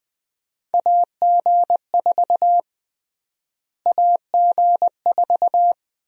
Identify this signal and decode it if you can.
Morse code, heard as a keyed tone, about 20 wpm, 700 Hz. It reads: AG4 AG4
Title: USA Call Sign Prefixes - Repeated 2x - 20wpm